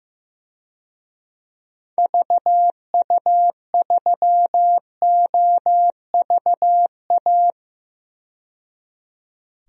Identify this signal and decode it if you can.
Morse code, heard as a keyed tone, about 15 wpm, 700 Hz. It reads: VU3OVA
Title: DXCC-100 Call Signs - 6 Characters - 15wpm